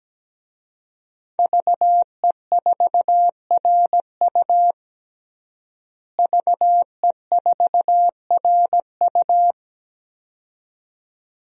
Morse code, heard as a keyed tone, about 17 wpm, 700 Hz. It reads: VE4RU VE4RU
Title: DXCC-100 Call Signs - 5 Characters - Repeated 2x - 17wpm